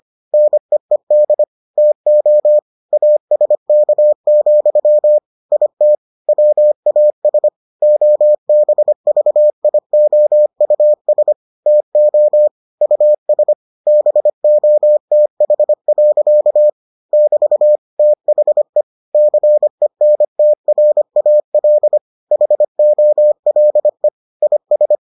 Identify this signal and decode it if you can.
Morse code, heard as a keyed tone, about 25 wpm, 600 Hz. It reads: NEED TO ASK, IT WAS OBVIOUS TO US BOTH. <BT> THE CENTRAL HOLE IS